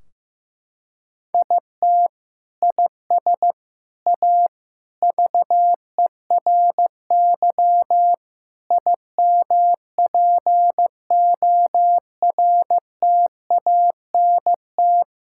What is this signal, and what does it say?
Morse code, heard as a keyed tone, about 15 wpm, 700 Hz. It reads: IT IS A VERY IMPORTANT
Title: Sentences from Top 1600 Words 15wpm